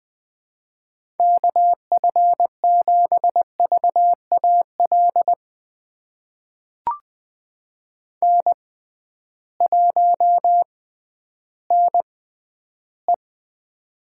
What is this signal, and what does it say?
Morse code, heard as a keyed tone, about 20 wpm, spaced as for 12 wpm, 700 Hz. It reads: KF7VAL N1NE
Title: USA Call Signs - 6x Character Spacing - 20wpm